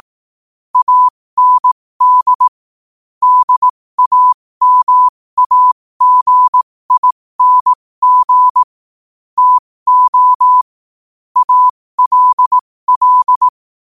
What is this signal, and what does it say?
Morse code, heard as a keyed tone, about 18 wpm, spaced as for 15 wpm, 1000 Hz. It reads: AND DAMAGING TO ALL